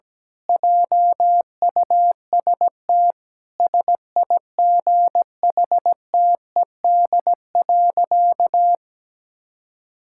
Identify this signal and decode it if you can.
Morse code, heard as a keyed tone, about 17 wpm, 700 Hz. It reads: JUST SIGHTED.
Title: Pangram Challenge v2 - 17wpm